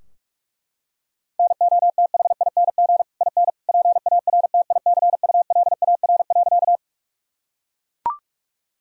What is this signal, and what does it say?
Morse code, heard as a keyed tone, about 41 wpm, 700 Hz. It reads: NOTHING IN PARTICULAR.
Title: Sentences from Top 1000 Words - Review Part 4 - 45wpm